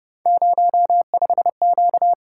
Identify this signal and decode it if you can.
Morse code, heard as a keyed tone, about 29 wpm, 700 Hz. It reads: 05Q